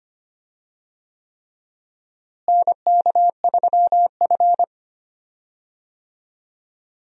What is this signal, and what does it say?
Morse code, heard as a keyed tone, about 25 wpm, 700 Hz. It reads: NK3F
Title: USA Call Signs - 4 Characters - Repeated 3x - 25wpm